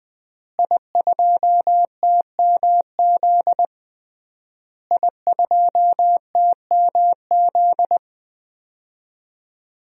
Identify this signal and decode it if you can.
Morse code, heard as a keyed tone, about 21 wpm, 700 Hz. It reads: I2TMZ I2TMZ
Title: DXCC-100 Call Signs - 5 Characters - Repeated 3x - 20wpm